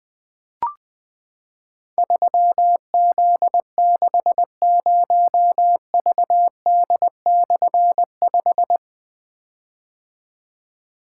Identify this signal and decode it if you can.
Morse code, heard as a keyed tone, about 20 wpm, 700 Hz. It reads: 3Z60VD/5
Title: Difficult Call Signs 20wpm